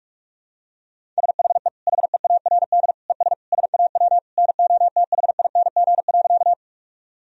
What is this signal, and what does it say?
Morse code, heard as a keyed tone, about 41 wpm, 700 Hz. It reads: SHE HEARD ES SAW NOTHING.
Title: Sentences from Top 500 Words and Abbreviations 45wpm